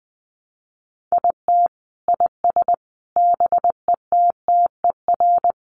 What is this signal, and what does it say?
Morse code, heard as a keyed tone, about 20 wpm, 700 Hz. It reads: IT IS BETTER
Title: Sentences from Top 1200 Words 20wpm